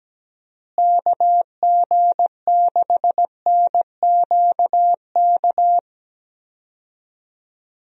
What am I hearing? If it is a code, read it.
Morse code, heard as a keyed tone, about 17 wpm, 700 Hz. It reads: KG6NQK
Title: USA Call Signs - 6 Characters - Repeated 2x - 17wpm